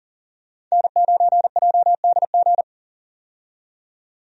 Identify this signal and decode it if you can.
Morse code, heard as a keyed tone, about 40 wpm, 700 Hz. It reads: N9JDG
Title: USA Call Signs - 5 Characters - 40wpm